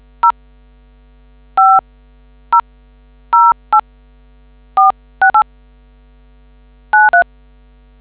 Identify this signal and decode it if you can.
DTMF, heard as telephone touch tones, tones of about 120 ms. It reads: 0500846893